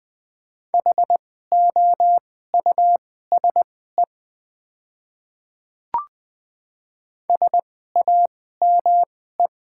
Morse code, heard as a keyed tone, about 20 wpm, spaced as for 14 wpm, 700 Hz. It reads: HOUSE SAME
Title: 2x Character Spacing - Top 200 Words - 20wpm